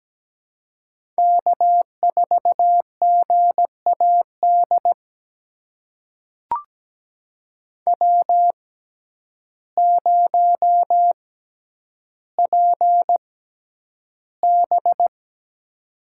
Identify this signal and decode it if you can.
Morse code, heard as a keyed tone, about 17 wpm, spaced as for 9 wpm, 700 Hz. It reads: K4GAD W0PB
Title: USA Call Signs - 6x Character Spacing - 17wpm